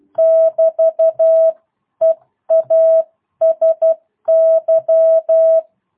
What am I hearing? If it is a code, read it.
Morse code, heard as a keyed tone, about 12 wpm, 650 Hz. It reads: <BT> EASY
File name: jh8cbh / 2022cw